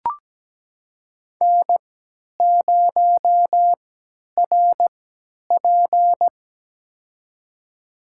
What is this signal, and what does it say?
Morse code, heard as a keyed tone, about 17 wpm, spaced as for 10 wpm, 700 Hz. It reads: N0RP
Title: USA Call Signs - 4 Characters - 3x Character Spacing - 17wpm